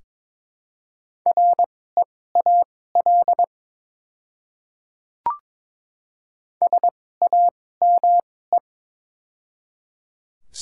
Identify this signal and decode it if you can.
Morse code, heard as a keyed tone, about 21 wpm, spaced as for 16 wpm, 700 Hz. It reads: REAL SAME
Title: 2x Character Spacing - Top 300 Words - 22wpm